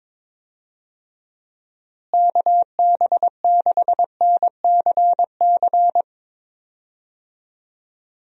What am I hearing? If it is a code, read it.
Morse code, heard as a keyed tone, about 22 wpm, 700 Hz. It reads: KB6NCC
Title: USA Call Signs - 6 Characters - Repeated 2x - 22wpm